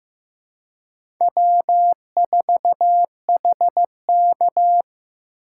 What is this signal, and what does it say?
Morse code, heard as a keyed tone, about 15 wpm, 700 Hz. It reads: W4HK
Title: USA Call Signs - 4 Characters - 3x Character Spacing - 15wpm